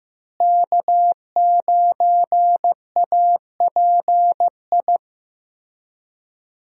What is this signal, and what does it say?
Morse code, heard as a keyed tone, about 15 wpm, 700 Hz. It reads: K9API
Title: USA Call Signs - 5 Characters - Repeated 3x - 15wpm